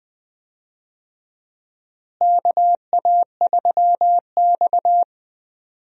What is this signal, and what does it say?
Morse code, heard as a keyed tone, about 20 wpm, 700 Hz. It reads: KA3X
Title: USA Call Signs - 4 Characters - Repeated 3x - 20wpm